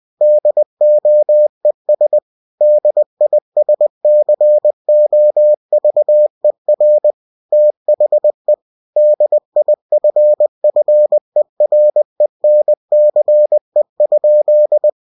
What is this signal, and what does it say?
Morse code, heard as a keyed tone, about 20 wpm, 600 Hz. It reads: DOES DISCOVER THE DIFFERENCE?
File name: cw-books / winnie-the-pooh / winnie-20wpm-page-0064